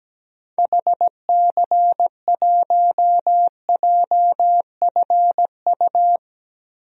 Morse code, heard as a keyed tone, about 17 wpm, 700 Hz. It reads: HC1JFU
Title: DXCC-100 Call Signs - 6 Characters - Repeated 3x - 17wpm